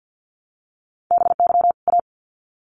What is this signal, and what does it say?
Morse code, heard as a keyed tone, about 50 wpm, 700 Hz. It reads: BY A
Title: Sets of 2 Words 50wpm